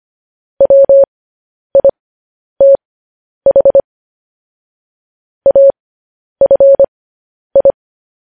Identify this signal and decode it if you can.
Morse code, heard as a keyed tone, about 25 wpm, spaced as for 10 wpm, 550 Hz. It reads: WITH AFI
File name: podcast / News headlines-1x-25x10